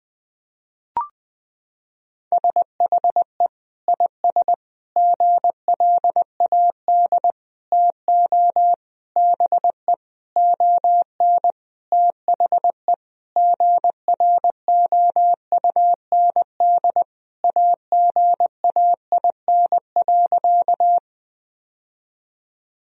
Morse code, heard as a keyed tone, about 20 wpm, 700 Hz. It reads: SHE IS GLAD TO BE ON THE GROUND AGAIN.
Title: Sentences from Top 1000 Words - Review Part 2 - 20wpm